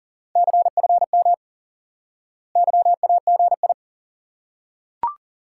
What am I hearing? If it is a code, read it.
Morse code, heard as a keyed tone, about 40 wpm, 700 Hz. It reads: CFM YAGI
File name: IQE Course 3 - Lesson 89 - YAGI - Two Elements - 40wpm